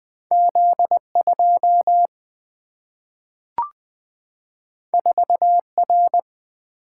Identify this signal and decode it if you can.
Morse code, heard as a keyed tone, about 20 wpm, 700 Hz. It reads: Z2 4R
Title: Single-Letter Single-Number 20wpm